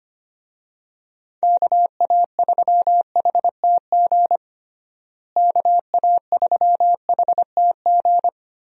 Morse code, heard as a keyed tone, about 25 wpm, 700 Hz. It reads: KA3HTG KA3HTG
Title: USA Call Signs - 6 Characters - Repeated 3x - 25wpm